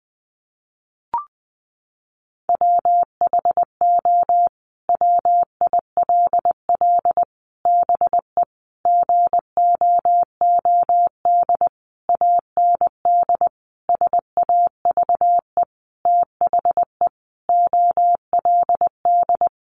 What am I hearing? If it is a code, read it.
Morse code, heard as a keyed tone, about 20 wpm, 700 Hz. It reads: WHO WILL BE GOOD AND SAVE THE OLD
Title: Sentences from Top 1100 Words 20wpm